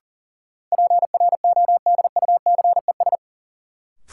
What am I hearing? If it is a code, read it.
Morse code, heard as a keyed tone, about 40 wpm, 700 Hz. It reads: PRODUCES
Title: Sentences from Top 1000 Words - Review Part 6 - New Word Drill - 40wpm